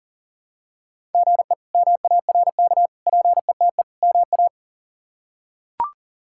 Morse code, heard as a keyed tone, about 40 wpm, 690 Hz. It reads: GE MARK PETE MA
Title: K1USN SST QSO Practice - 40wpm